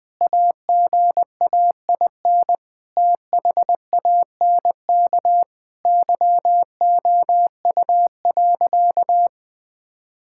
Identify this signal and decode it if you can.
Morse code, heard as a keyed tone, about 20 wpm, 700 Hz. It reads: AGAIN THANK YOU.
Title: Sentences with 8-Word Phrases - 20wpm